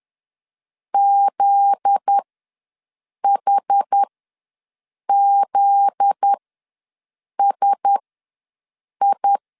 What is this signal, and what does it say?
Morse code, heard as a keyed tone, about 10 wpm, spaced as for 6 wpm, 800 Hz. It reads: ZHZSI
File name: LEZIONI / 7 GF9Z / 7 GF9Z ws 10wpm 3919